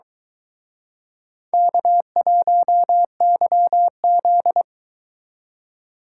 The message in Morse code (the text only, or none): K1YZ